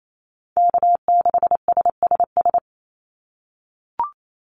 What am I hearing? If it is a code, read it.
Morse code, heard as a keyed tone, about 28 wpm, 700 Hz. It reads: K6SSS
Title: Call Signs - 28wpm